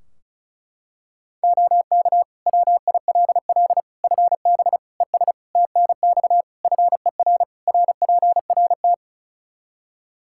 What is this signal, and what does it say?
Morse code, heard as a keyed tone, about 34 wpm, 700 Hz. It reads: OK WILL FB ES TNX FER RPRT
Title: Practice QSO - Exchange 2 - Info - 35wpm